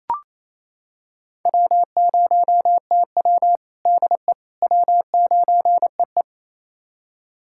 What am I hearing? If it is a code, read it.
Morse code, heard as a keyed tone, about 28 wpm, 700 Hz. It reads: W0TW DE W9EE
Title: USA Callsign DE Callsign - 4 Characters - 28wpm